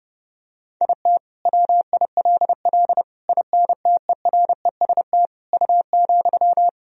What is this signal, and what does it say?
Morse code, heard as a keyed tone, about 32 wpm, 700 Hz. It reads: IT WILL INTEREST U,
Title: Sentences from Top 400 Words and Abbreviations 30wpm